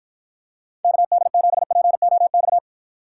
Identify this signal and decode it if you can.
Morse code, heard as a keyed tone, about 49 wpm, 700 Hz. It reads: KD7POX